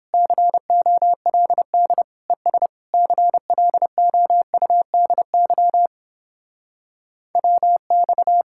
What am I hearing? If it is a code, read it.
Morse code, heard as a keyed tone, about 31 wpm, 700 Hz. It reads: COLD ES CLOUDY WX